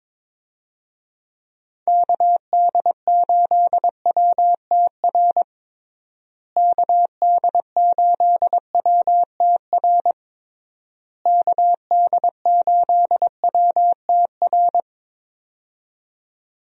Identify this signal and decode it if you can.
Morse code, heard as a keyed tone, about 21 wpm, 700 Hz. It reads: KD8WTR KD8WTR KD8WTR
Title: USA Call Signs - 6 Characters - Repeated 3x - 22wpm